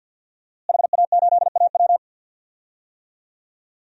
Morse code, heard as a keyed tone, about 52 wpm, 700 Hz. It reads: HA8AW
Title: DXCC-100 Call Signs - 5 Characters - Repeated 3x - 50wpm